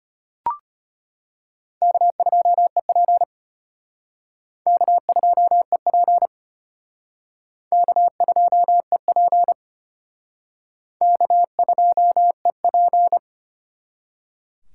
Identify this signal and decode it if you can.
Morse code, heard as a keyed tone, about 31 wpm, 700 Hz. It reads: K2EP K2EP K2EP K2EP
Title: Speed Racing - USA 1x2 Call Signs 25wpm